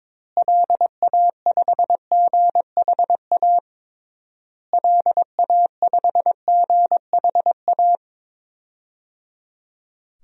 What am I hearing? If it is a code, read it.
Morse code, heard as a keyed tone, about 21 wpm, 700 Hz. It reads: LA5GHA LA5GHA
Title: DXCC-100 Call Signs - 6 Characters - Repeated 3x - 22wpm